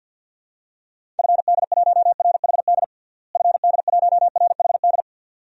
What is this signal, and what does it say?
Morse code, heard as a keyed tone, about 47 wpm, 700 Hz. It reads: FD1RHD FD1RHD
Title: DXCC-100 Call Signs - 6 Characters - Repeated 2x - 50wpm